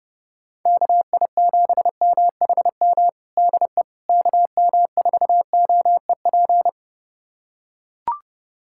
Morse code, heard as a keyed tone, about 32 wpm, 700 Hz. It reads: KI7MHM DE KM4OEP
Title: USA Callsign DE Callsign - 6 Characters - 30wpm